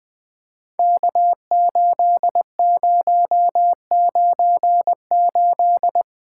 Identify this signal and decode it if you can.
Morse code, heard as a keyed tone, about 21 wpm, 700 Hz. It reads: K8098
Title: POTA QSO Practice - 20wpm